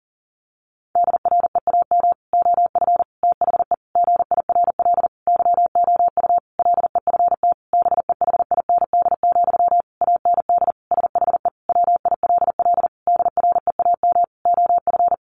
Morse code, heard as a keyed tone, about 40 wpm, 700 Hz. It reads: DREAM OF THE GIRL YOU LEFT BEHIND, AND SHE WILL DREAM OF
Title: Sentences from Top 1500 Words 40wpm